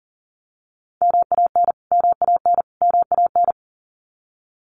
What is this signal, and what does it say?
Morse code, heard as a keyed tone, about 40 wpm, 700 Hz. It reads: MAN MAN MAN
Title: Top 300 Words Repeated 40wpm